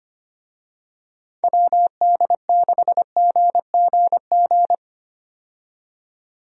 Morse code, heard as a keyed tone, about 25 wpm, 700 Hz. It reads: WD6GGG
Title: USA Call Signs - 6 Characters - 25wpm